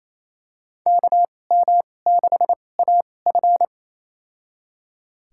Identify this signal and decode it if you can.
Morse code, heard as a keyed tone, about 27 wpm, spaced as for 20 wpm, 700 Hz. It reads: KM6AF
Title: USA Call Signs - 2x Character Spacing - 28wpm